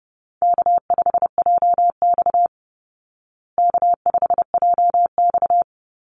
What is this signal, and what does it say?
Morse code, heard as a keyed tone, about 30 wpm, 700 Hz. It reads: K5JX K5JX
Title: Easy Call Signs Repeated 2x - 30wpm